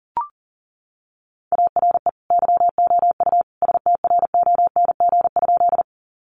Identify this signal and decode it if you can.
Morse code, heard as a keyed tone, about 40 wpm, 700 Hz. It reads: ARE YOU STRONG?
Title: Sentences from Top 500 Words 40wpm